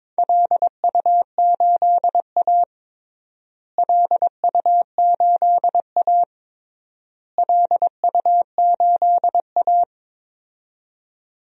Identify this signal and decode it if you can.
Morse code, heard as a keyed tone, about 21 wpm, 700 Hz. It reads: LU8A LU8A LU8A
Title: DXCC-100 Call Signs - 4 Characters - Repeated 3x - 22wpm